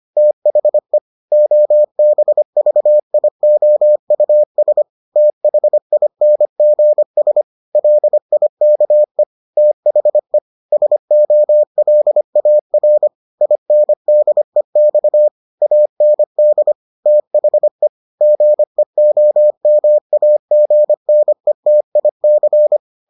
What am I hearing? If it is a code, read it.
Morse code, heard as a keyed tone, about 25 wpm, 600 Hz. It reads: THE OBVIOUS THINGS LIKE THE SOLAR INDEX AND THE GEOMAGNETIC